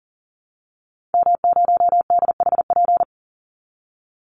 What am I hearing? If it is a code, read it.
Morse code, heard as a keyed tone, about 40 wpm, 700 Hz. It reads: M0DHP
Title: Call Signs Repeated 2x - 40wpm